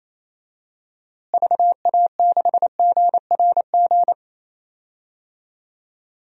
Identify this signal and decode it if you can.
Morse code, heard as a keyed tone, about 28 wpm, 700 Hz. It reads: VA6GRG
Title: DXCC-100 Call Signs - 6 Characters - Repeated 3x - 28wpm